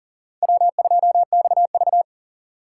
Morse code, heard as a keyed tone, about 39 wpm, 700 Hz. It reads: W2XV
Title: USA Call Signs - 4 Characters - 8x Character Spacing - 40wpm